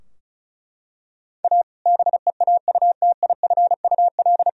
Morse code, heard as a keyed tone, about 34 wpm, 700 Hz. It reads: A BEAUTIFUL
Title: Sentences from Top 1000 Words - Review Part 2 - 35wpm